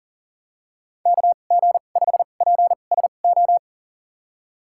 Morse code, heard as a keyed tone, about 40 wpm, spaced as for 29 wpm, 700 Hz. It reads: KG5PSO
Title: USA Call Signs - 6 Characters - 2x Character Spacing - 40wpm